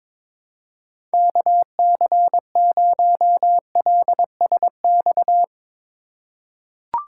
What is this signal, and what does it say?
Morse code, heard as a keyed tone, about 21 wpm, 700 Hz. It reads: KC0LSX
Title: USA Call Signs - 4x Character Spacing - 22wpm